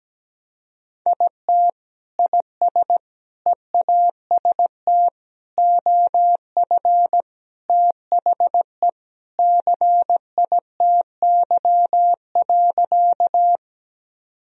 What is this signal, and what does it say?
Morse code, heard as a keyed tone, about 17 wpm, 700 Hz. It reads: IT IS EAST OF THE CITY.
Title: Sentences from Top 1000 Words - Review Part 3 - 17wpm